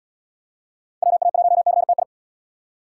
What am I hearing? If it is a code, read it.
Morse code, heard as a keyed tone, about 67 wpm, 700 Hz. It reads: WA1CTI